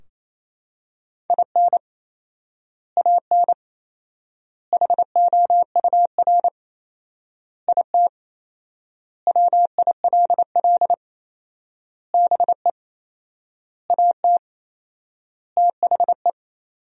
Morse code, heard as a keyed tone, about 28 wpm, 700 Hz. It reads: IN AN HOUR IT WILL BE AT THE
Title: 4x Word-Spacing - Sentences from Top 400 Words - 28wpm